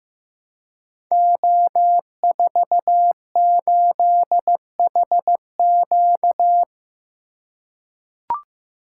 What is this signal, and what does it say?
Morse code, heard as a keyed tone, about 15 wpm, 700 Hz. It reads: O48HQ
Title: Code Groups - 5 Letter:Numbers - 15wpm